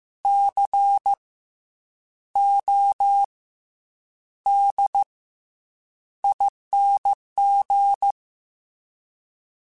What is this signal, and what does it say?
Morse code, heard as a keyed tone, about 15 wpm, spaced as for 8 wpm, 800 Hz. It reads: CODING
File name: Special INGs-mp3s / Special INGs-mp3s / Fast ing @13